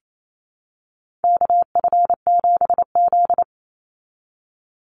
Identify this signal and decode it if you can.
Morse code, heard as a keyed tone, about 28 wpm, 700 Hz. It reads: KF7Z
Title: Call Signs Repeated 3x - 28wpm